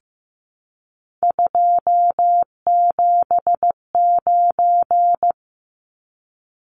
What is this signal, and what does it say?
Morse code, heard as a keyed tone, about 15 wpm, 700 Hz. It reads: 279
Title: Numbers - 3-Digit 15wpm